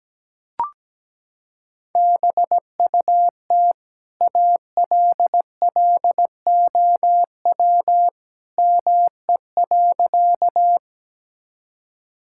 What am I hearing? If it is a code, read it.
Morse code, heard as a keyed tone, about 17 wpm, 700 Hz. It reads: BUT ALLOW ME.
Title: Sentences from Top 1000 Words - Review Part 5 - 17wpm